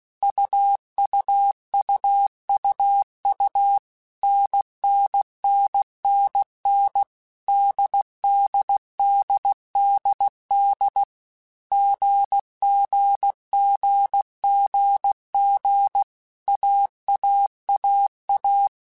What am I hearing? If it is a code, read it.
Morse code, heard as a keyed tone, about 16 wpm, 800 Hz. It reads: UUUUU NNNNN DDDDD GGGGG AAAA
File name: Basic Letter Groups By7Groups / AWUNDG / AWUNDG@15